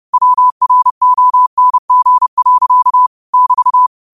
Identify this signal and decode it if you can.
Morse code, heard as a keyed tone, about 30 wpm, 1000 Hz. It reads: WRONG. <BT>
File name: qotdfiles / 30wpm